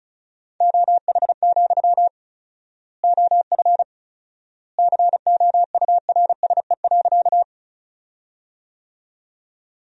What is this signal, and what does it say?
Morse code, heard as a keyed tone, about 34 wpm, 700 Hz. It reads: OH, OF COURSE.
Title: 4x Word-Spacing - Sentences from Top 500 Words - 35wpm